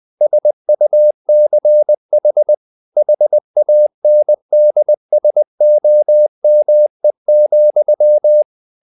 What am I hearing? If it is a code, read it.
Morse code, heard as a keyed tone, about 20 wpm, 600 Hz. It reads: SUCH HANDSOME,
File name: the old man and the sea part 2 20wpm